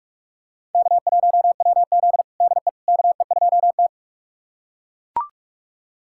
Morse code, heard as a keyed tone, about 41 wpm, 700 Hz. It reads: K1WZ DE KE2T
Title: USA Callsign DE Callsign - 4 Characters - 45wpm